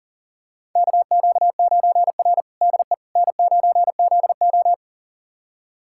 Morse code, heard as a keyed tone, about 40 wpm, 700 Hz. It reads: KQ9R DE N9ZO
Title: USA Callsign DE Callsign - 4 Characters - 40wpm